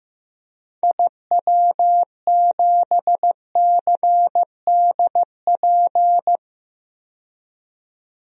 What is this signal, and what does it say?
Morse code, heard as a keyed tone, about 15 wpm, 700 Hz. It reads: IW7CDP
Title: DXCC-100 Call Signs - 6 Characters - 15wpm